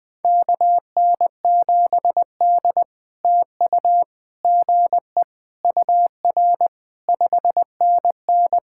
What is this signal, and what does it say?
Morse code, heard as a keyed tone, about 20 wpm, 700 Hz. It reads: KN7D TU GE UR 5NN
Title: POTA QSO Practice - 20wpm